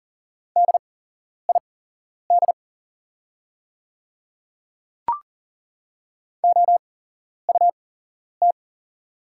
Morse code, heard as a keyed tone, about 40 wpm, spaced as for 11 wpm, 690 Hz. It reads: DID OUT
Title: 8x Character Spacing - Top 200 Words - 40wpm